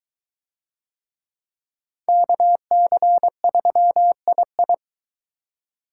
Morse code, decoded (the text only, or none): KC3II